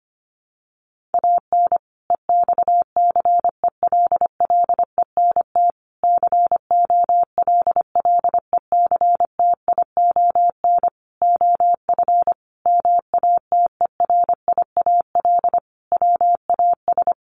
Morse code, heard as a keyed tone, about 25 wpm, 700 Hz. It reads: AN EXCELLENT COLLECTION OF MATERIAL WAS
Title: Sentences from Top 1200 Words 25wpm